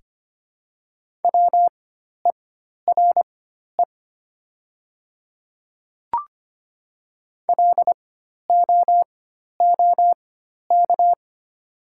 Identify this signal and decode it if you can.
Morse code, heard as a keyed tone, about 25 wpm, spaced as for 12 wpm, 700 Hz. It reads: WERE LOOK
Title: 4x Character Spacing - Top 100 Words - 25wpm